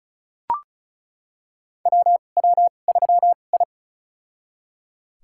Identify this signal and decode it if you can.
Morse code, heard as a keyed tone, about 36 wpm, spaced as for 25 wpm, 700 Hz. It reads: WW3I
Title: USA Call Signs - 2x Character Spacing - 35wpm